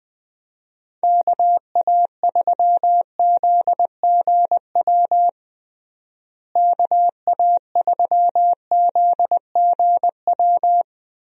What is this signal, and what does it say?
Morse code, heard as a keyed tone, about 20 wpm, 700 Hz. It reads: KA3ZGW KA3ZGW
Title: USA Call Signs - 6 Characters - Repeated 3x - 20wpm